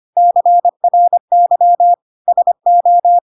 Morse code, heard as a keyed tone, about 25 wpm, 700 Hz. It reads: CRY SO
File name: Pinocchio / Pinocchio ch7 25 25 ik1hge